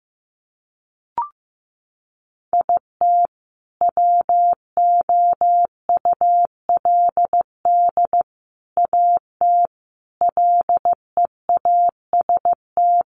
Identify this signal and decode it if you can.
Morse code, heard as a keyed tone, about 14 wpm, 700 Hz. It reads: IT WOULD AT LEAST
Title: Sentences from Top 700 Words 15wpm